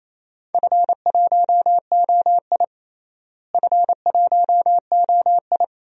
Morse code, heard as a keyed tone, about 28 wpm, 700 Hz. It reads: F1OI F1OI
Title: DXCC-100 Call Signs - 4 Characters - Repeated 2x - 28wpm